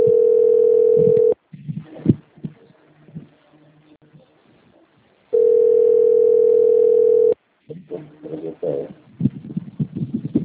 Ringback tone (North America), 440 + 480 Hz.
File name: out- - - - - .2849